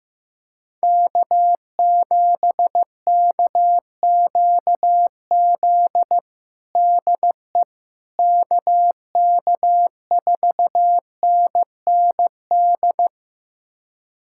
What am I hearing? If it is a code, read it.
Morse code, heard as a keyed tone, about 15 wpm, 700 Hz. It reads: K7KQZ DE KK4NND
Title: USA Callsign DE Callsign 15wpm